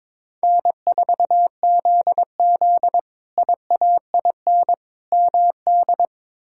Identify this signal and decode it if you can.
Morse code, heard as a keyed tone, about 22 wpm, 700 Hz. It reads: N4ZZ IAIN MD
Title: K1USN SST QSO Practice - 22wpm